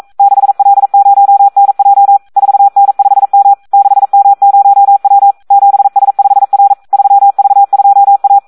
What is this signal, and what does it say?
Morse code, heard as a keyed tone, about 42 wpm, 800 Hz. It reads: 6L0NJ 4N5M 6M0W 7S5R 3V2A
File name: Audio practica CW / SD 1 42